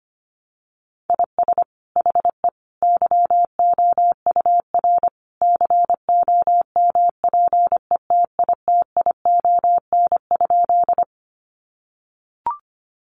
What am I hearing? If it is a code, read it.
Morse code, heard as a keyed tone, about 25 wpm, 700 Hz. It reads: IS HE YOUR COMPETITION?
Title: Sentences from Top 1000 Words 25wpm